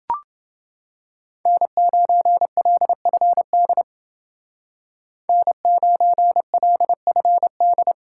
Morse code, heard as a keyed tone, about 29 wpm, 700 Hz. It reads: N9LFD N9LFD